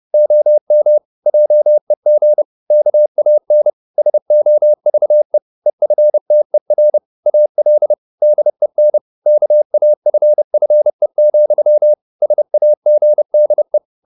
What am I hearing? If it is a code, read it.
Morse code, heard as a keyed tone, about 30 wpm, 600 Hz. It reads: OM JEG KAN SOVE EFTER AL DEN KAFFE, SAGDE
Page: Ferie 30 wpm